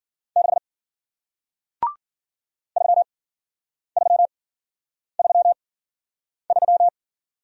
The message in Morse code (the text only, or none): 6 3333